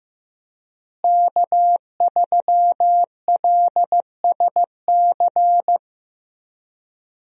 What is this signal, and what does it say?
Morse code, heard as a keyed tone, about 15 wpm, 700 Hz. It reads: K3LSC
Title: USA Call Signs - 5 Characters - Repeated 3x - 15wpm